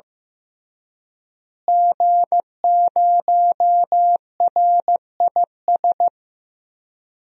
Morse code, heard as a keyed tone, about 16 wpm, 700 Hz. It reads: G0RIS
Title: DXCC-100 Call Signs - 5 Characters - Repeated 3x - 15wpm